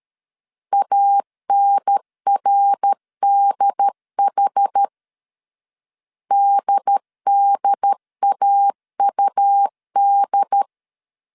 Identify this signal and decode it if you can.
Morse code, heard as a keyed tone, about 13 wpm, 800 Hz. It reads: ANRDH DDAUD
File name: CW / DET / 1 3 1500C S C 12wpm 2750